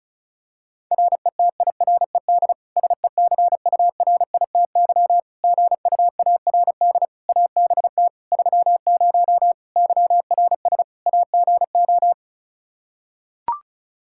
Morse code, heard as a keyed tone, about 34 wpm, 700 Hz. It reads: RETIRED SECURITY GUARD ABT 30 YRS AGO
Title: K1USN SST QSO Practice 35wpm